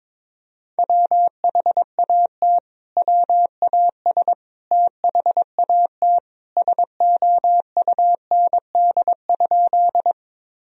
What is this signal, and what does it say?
Morse code, heard as a keyed tone, about 22 wpm, 700 Hz. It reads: WHAT WAS THAT SOUND?
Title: Sentences from Top 1000 Words - Review Part 1 - 22wpm